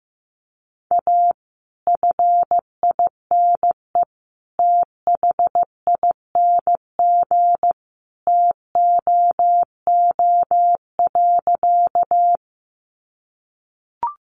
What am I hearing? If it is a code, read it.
Morse code, heard as a keyed tone, about 15 wpm, 700 Hz. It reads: A FINE THING TOO.
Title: Sentences from Top 500 Words 15wpm